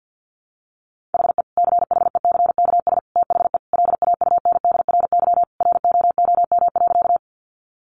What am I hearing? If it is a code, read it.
Morse code, heard as a keyed tone, about 50 wpm, 700 Hz. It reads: HE CHECKS THE LAUNDRY ROOM.
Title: Easy Phrases 50wpm